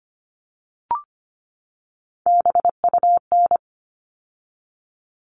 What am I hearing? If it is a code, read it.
Morse code, heard as a keyed tone, about 25 wpm, 700 Hz. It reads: BUN
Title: All Cut Numbers - 3-Digit 25wpm